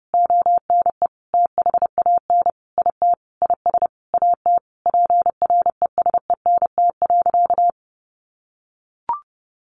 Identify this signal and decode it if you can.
Morse code, heard as a keyed tone, about 30 wpm, 700 Hz. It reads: ONE THAN IT IS AT PRESENT.
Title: Sentences from Top 1200 Words 30wpm